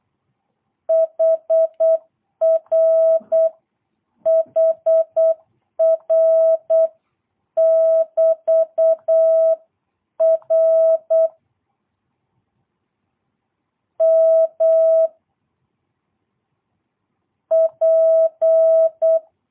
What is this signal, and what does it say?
Morse code, heard as a keyed tone, about 8 wpm, 660 Hz. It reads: HR HR <BT> R M P
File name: jh8cbh / 2022cw